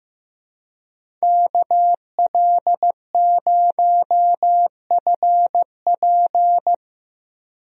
Morse code, heard as a keyed tone, about 15 wpm, 700 Hz. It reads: KL0FP
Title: USA Call Signs - 5 Characters - Repeated 2x - 15wpm